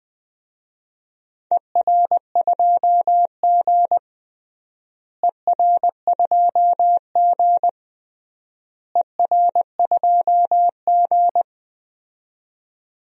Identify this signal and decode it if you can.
Morse code, heard as a keyed tone, about 21 wpm, 700 Hz. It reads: ER2G ER2G ER2G
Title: DXCC-100 Call Signs - 4 Characters - Repeated 3x - 20wpm